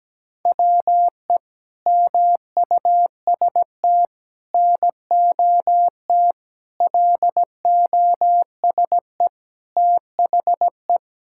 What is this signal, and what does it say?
Morse code, heard as a keyed tone, about 17 wpm, 700 Hz. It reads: WE MUST NOT LOSE THE
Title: Sentences from Top 1600 Words 17wpm